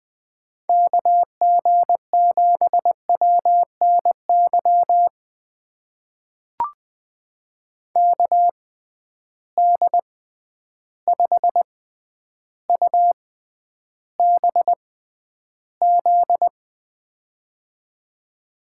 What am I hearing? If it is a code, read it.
Morse code, heard as a keyed tone, about 20 wpm, spaced as for 10 wpm, 700 Hz. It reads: KG7WNY KD5UBZ
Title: USA Call Signs - 6x Character Spacing - 20wpm